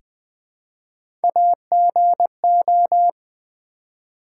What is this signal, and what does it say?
Morse code, heard as a keyed tone, about 20 wpm, 700 Hz. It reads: AGO
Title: Three Letter Words - 20wpm